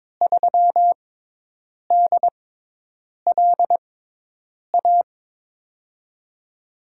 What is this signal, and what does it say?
Morse code, heard as a keyed tone, about 23 wpm, spaced as for 8 wpm, 700 Hz. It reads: 3DLA
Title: USA Call Signs - 6 Characters - 6x Character Spacing - 22wpm